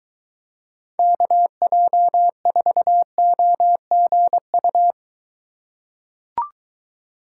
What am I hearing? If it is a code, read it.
Morse code, heard as a keyed tone, about 23 wpm, 700 Hz. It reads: KJ4OGU